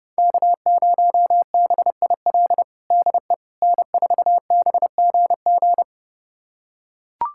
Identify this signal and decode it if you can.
Morse code, heard as a keyed tone, about 30 wpm, 700 Hz. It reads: K0BIL DE N4BGG
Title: USA Callsign DE Callsign - 5 Characters - 30wpm